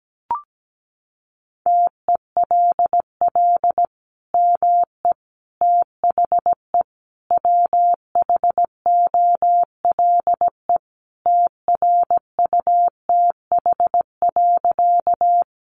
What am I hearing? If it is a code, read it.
Morse code, heard as a keyed tone, about 17 wpm, 700 Hz. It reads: TELL ME THE WHOLE TRUTH.
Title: Sentences from Top 800 Words 17wpm